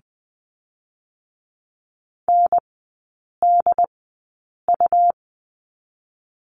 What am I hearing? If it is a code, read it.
Morse code, heard as a keyed tone, about 20 wpm, spaced as for 8 wpm, 700 Hz. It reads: NDU
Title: 100 Lesson 17e - Letter u - Three Characters 20wpm